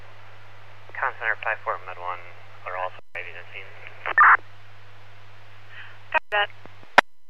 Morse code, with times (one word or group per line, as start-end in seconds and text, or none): none